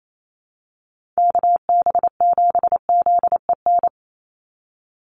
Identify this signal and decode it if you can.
Morse code, heard as a keyed tone, about 28 wpm, 700 Hz. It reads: KB7ZEN
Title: USA Call Signs 28wpm